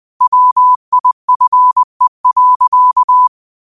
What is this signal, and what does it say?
Morse code, heard as a keyed tone, about 20 wpm, 1000 Hz. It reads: WIFE.